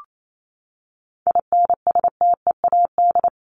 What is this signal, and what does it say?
Morse code, heard as a keyed tone, about 28 wpm, 700 Hz. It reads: INSTEAD
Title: Top 501-750 Words 28wpm